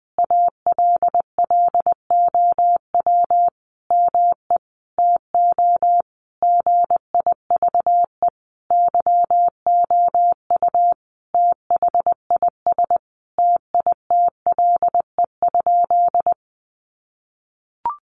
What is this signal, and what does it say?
Morse code, heard as a keyed tone, about 19 wpm, 700 Hz. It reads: ALLOW ME TO GIVE YOU THIS TITLE?
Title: Sentences from Top 1200 Words 20wpm